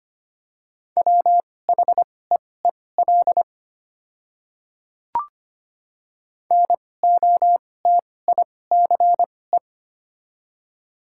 Morse code, heard as a keyed tone, about 25 wpm, spaced as for 18 wpm, 700 Hz. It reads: WHEEL NOTICE
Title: 2x Character Spacing - Top 500 Words - 25wpm